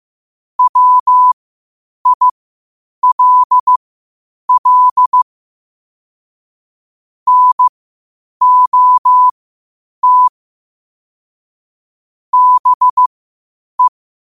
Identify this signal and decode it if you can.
Morse code, heard as a keyed tone, about 15 wpm, spaced as for 8 wpm, 1000 Hz. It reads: WILL NOT BE
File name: qotdfiles / 8wpm